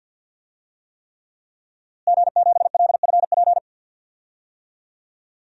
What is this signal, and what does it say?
Morse code, heard as a keyed tone, about 52 wpm, 700 Hz. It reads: G7LFP
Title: DXCC-100 Call Signs - 5 Characters - Repeated 3x - 50wpm